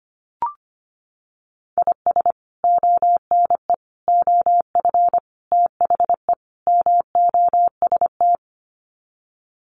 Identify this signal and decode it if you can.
Morse code, heard as a keyed tone, about 25 wpm, 700 Hz. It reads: IS ONE OF THE MOST
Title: Sets of 5 Words 25wpm